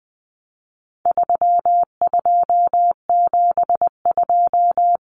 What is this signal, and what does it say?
Morse code, heard as a keyed tone, about 20 wpm, 700 Hz. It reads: 3272
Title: Common Cut Numbers - 4-Digit 20wpm